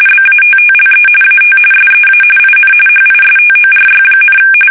RTTY, about 45.45 baud. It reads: BATTLE AT SEA IN HISTORY,
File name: fsk850 / battleshipnj